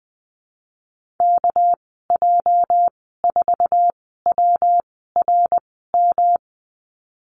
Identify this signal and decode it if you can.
Morse code, heard as a keyed tone, about 19 wpm, spaced as for 14 wpm, 700 Hz. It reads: KJ4WRM
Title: 207 Lesson 33g - Letter j - Call Signs 20wpm